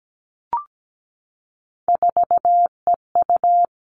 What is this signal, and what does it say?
Morse code, heard as a keyed tone, about 17 wpm, 700 Hz. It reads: 4EU
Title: All Cut Numbers - 3-Digit 17wpm